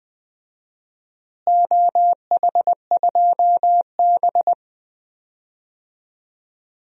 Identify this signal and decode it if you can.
Morse code, heard as a keyed tone, about 20 wpm, 700 Hz. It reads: OH2B
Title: DXCC-100 Call Signs - 4 Characters - Repeated 2x - 20wpm